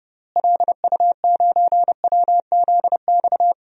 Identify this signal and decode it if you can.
Morse code, heard as a keyed tone, about 30 wpm, 700 Hz. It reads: LU9WZX
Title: DXCC-100 Call Signs - 6 Characters - Repeated 3x - 30wpm